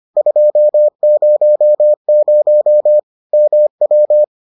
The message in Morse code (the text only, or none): 200 MW